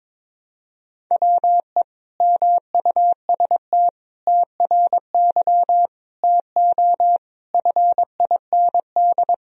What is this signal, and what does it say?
Morse code, heard as a keyed tone, about 22 wpm, 700 Hz. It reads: WE MUST TRY TO FIND
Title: Sentences from Top 400 Words and Abbreviations 22wpm